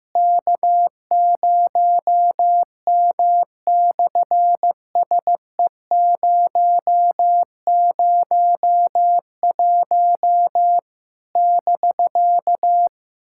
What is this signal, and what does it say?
Morse code, heard as a keyed tone, about 14 wpm, 700 Hz. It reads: K0M/SE001 <BK>
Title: SOTA QSO Practice 15wpm